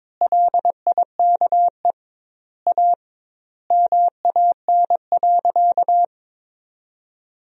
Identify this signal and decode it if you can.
Morse code, heard as a keyed tone, about 22 wpm, 700 Hz. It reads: LIKE A MAN.
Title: 2x Word-Spacing - Sentences from Top 300 Words - 22wpm